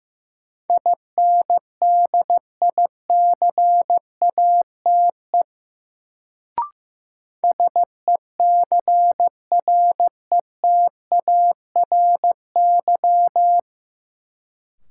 Morse code, heard as a keyed tone, about 15 wpm, 700 Hz. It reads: INDICATE SECRETARY
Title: Sentences from Top 1300 Words - New Word Drill Alt - 15wpm